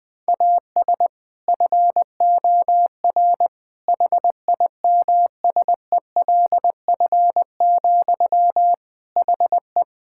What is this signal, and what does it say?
Morse code, heard as a keyed tone, about 20 wpm, 700 Hz. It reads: AS FOR HIMSELF, HE
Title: Sentences from Top 1600 Words 20wpm